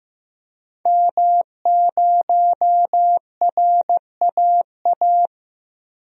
Morse code, heard as a keyed tone, about 15 wpm, 700 Hz. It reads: M0RAA
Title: DXCC-100 Call Signs - 5 Characters - 15wpm